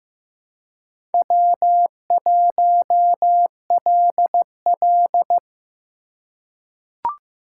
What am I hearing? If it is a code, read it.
Morse code, heard as a keyed tone, about 15 wpm, 700 Hz. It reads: W1LL
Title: USA Call Signs - 4 Characters - 2x Character Spacing - 15wpm